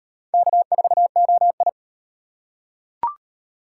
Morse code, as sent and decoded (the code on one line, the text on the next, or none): -.- ....- --- ..
K4OI